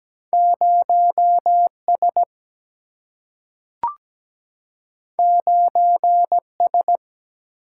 Morse code, heard as a keyed tone, about 17 wpm, 700 Hz. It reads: 0 S 9 S
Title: Single-Letter Single-Number 17wpm